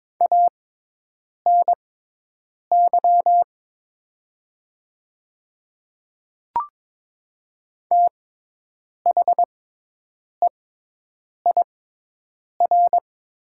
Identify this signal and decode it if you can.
Morse code, heard as a keyed tone, about 22 wpm, spaced as for 8 wpm, 700 Hz. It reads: ANY THEIR
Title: 6x Character Spacing - Top 500 Words - 22wpm